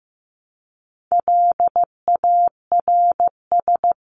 Morse code, heard as a keyed tone, about 15 wpm, 700 Hz. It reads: LARS
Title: CWT Exchange 15wpm